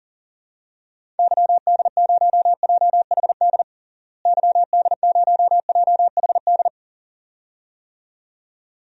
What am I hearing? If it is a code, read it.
Morse code, heard as a keyed tone, about 37 wpm, 700 Hz. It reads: YD0JHD YD0JHD
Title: DXCC-100 Call Signs - 6 Characters - Repeated 2x - 40wpm